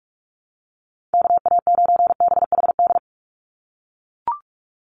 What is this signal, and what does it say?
Morse code, heard as a keyed tone, about 44 wpm, 700 Hz. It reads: KA9BHD
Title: Call Signs - 45wpm